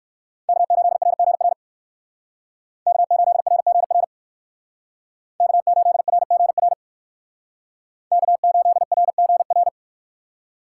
Morse code, heard as a keyed tone, about 58 wpm, 700 Hz. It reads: K8RGR K8RGR K8RGR K8RGR